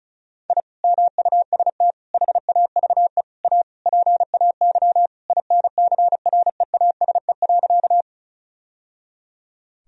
Morse code, heard as a keyed tone, about 36 wpm, 700 Hz. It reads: I MUST HAVE A PAY INCREASE.
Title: Sentences from Top 1000 Words - Review Part 4 - 35wpm